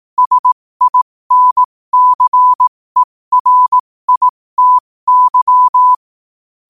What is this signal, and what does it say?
Morse code, heard as a keyed tone, about 18 wpm, spaced as for 15 wpm, 1000 Hz. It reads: SINCERITY
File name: qotdfiles / 15wpm